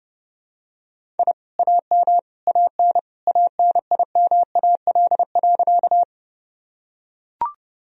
Morse code, heard as a keyed tone, about 30 wpm, 700 Hz. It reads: I AM AN ANIMAL.
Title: Sentences from Top 1000 Words - Review Part 1 - 30wpm